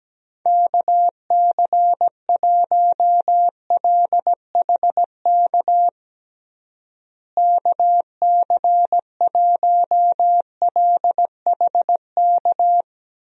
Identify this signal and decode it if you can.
Morse code, heard as a keyed tone, about 17 wpm, 700 Hz. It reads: KC1LHK KC1LHK
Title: USA Call Signs - 6 Characters - Repeated 3x - 17wpm